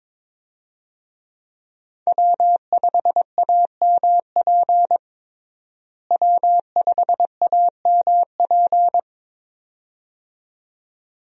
Morse code, heard as a keyed tone, about 22 wpm, 700 Hz. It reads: W5AMP W5AMP
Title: USA Call Signs - 5 Characters - Repeated 2x - 22wpm